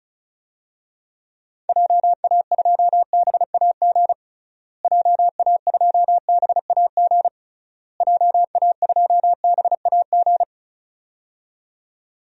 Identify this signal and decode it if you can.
Morse code, heard as a keyed tone, about 34 wpm, 700 Hz. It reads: JA2BAG JA2BAG JA2BAG
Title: DXCC-100 Call Signs - 6 Characters - Repeated 3x - 35wpm